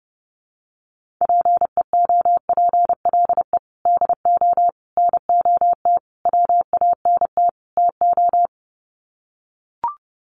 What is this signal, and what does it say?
Morse code, heard as a keyed tone, about 30 wpm, 700 Hz. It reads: PEOPLE DO NOT WANT TO
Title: Sets of 5 Words Encore2 30wpm